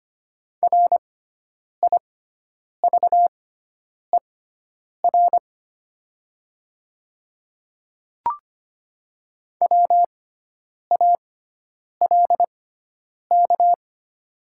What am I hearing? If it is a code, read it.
Morse code, heard as a keyed tone, about 25 wpm, spaced as for 9 wpm, 690 Hz. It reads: RIVER WALK
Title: 6x Character Spacing - Top 300 Words - 25wpm